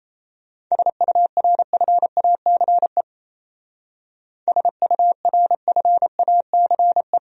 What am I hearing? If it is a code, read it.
Morse code, heard as a keyed tone, about 34 wpm, 700 Hz. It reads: SURFACE SURFACE
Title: Speed-Racing Top 500 Words 28wpm